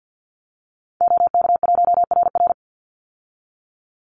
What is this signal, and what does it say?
Morse code, heard as a keyed tone, about 50 wpm, 700 Hz. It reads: OK1RR
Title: Call Signs - 50wpm